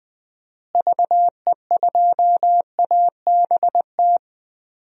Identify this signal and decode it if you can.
Morse code, heard as a keyed tone, about 20 wpm, 700 Hz. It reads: VE2ABT
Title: DXCC-100 Call Signs - 6 Characters - Repeated 3x - 20wpm